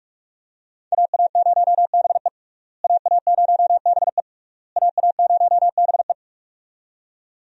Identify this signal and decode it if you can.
Morse code, heard as a keyed tone, about 42 wpm, 700 Hz. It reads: AA0BE AA0BE AA0BE
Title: USA Call Signs - 5 Characters - Repeated 3x - 45wpm